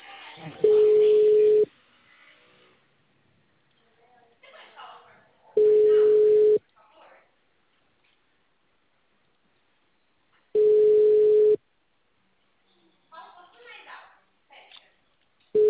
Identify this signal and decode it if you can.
Ringback tone (Europe, 425 Hz), with speech.
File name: out- -2004- - - .222